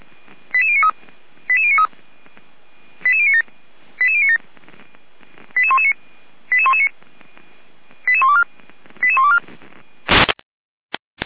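Five-tone selective call (ZVEI), heard as quick runs of five tones, tones of about 70 ms. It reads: 88093 88093 88097 88097 88119 88119 88124 88124
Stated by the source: ZVEI five-tone selective call received on SDR.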